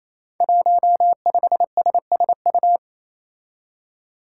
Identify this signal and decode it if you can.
Morse code, heard as a keyed tone, about 28 wpm, 700 Hz. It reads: 15SSU
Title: Five Challenging Letter-Number Combinations 28wpm